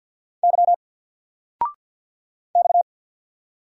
Morse code, heard as a keyed tone, about 50 wpm, 700 Hz. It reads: Y X
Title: Single Letter-Number 50wpm